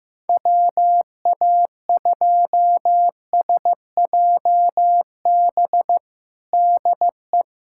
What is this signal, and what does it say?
Morse code, heard as a keyed tone, about 15 wpm, 700 Hz. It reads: WA2SJB DE
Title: USA Callsign DE Callsign - 6 Characters - 15wpm